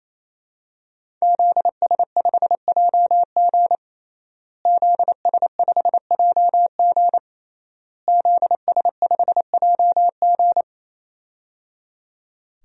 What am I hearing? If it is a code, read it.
Morse code, heard as a keyed tone, about 28 wpm, 700 Hz. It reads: ZS5JG ZS5JG ZS5JG
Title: DXCC-100 Call Signs - 5 Characters - Repeated 3x - 28wpm